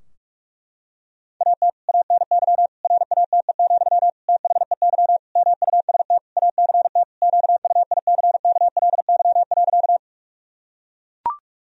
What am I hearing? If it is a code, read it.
Morse code, heard as a keyed tone, about 41 wpm, 700 Hz. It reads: AT ANY RATE, THEY MUST ACT QUICKLY.
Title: Sentences from Top 1000 Words - Review Part 4 - 45wpm